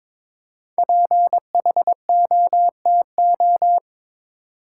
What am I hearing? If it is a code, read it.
Morse code, heard as a keyed tone, about 22 wpm, 700 Hz. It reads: PHOTO
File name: 5 Letter Words - First 3 Letters Given - 22wpm